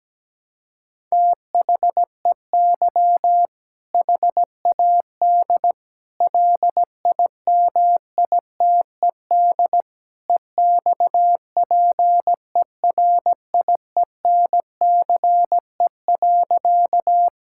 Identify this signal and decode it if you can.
Morse code, heard as a keyed tone, about 17 wpm, 700 Hz. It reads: THEY HAD LIMITED EXPERIENCE.
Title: Sentences from Top 1000 Words - Review Part 6 - 17wpm